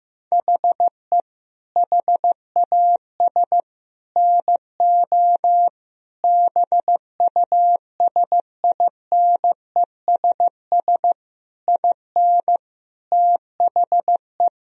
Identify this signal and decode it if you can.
Morse code, heard as a keyed tone, about 15 wpm, 700 Hz. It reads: HE HAS NO BUSINESS IN THE
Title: Sentences from Top 1000 Words - Review Part 4 - 15wpm